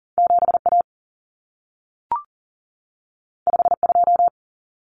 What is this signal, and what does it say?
Morse code, heard as a keyed tone, about 40 wpm, 700 Hz. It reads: 7A 52
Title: Common Cut Numbers - 2-Digit 40wpm